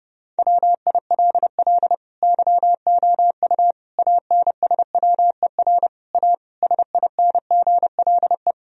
Morse code, heard as a keyed tone, about 30 wpm, 700 Hz. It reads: WILL YOU ANSWER A SINGLE
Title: Sentences from Top 1000 Words - Review Part 4 - 30wpm